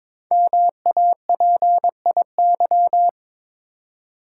Morse code, heard as a keyed tone, about 22 wpm, 700 Hz. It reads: MAPIY
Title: Code Groups - 5 Letters - 22wpm